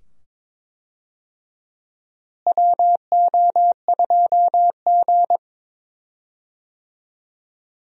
Morse code, heard as a keyed tone, about 22 wpm, 700 Hz. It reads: WO2G
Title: USA Call Signs - 4 Characters - Repeated 2x - 22wpm